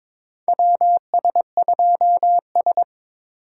Morse code, heard as a keyed tone, about 22 wpm, 700 Hz. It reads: WS2S
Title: USA Call Signs - 4 Characters - 8x Character Spacing - 22wpm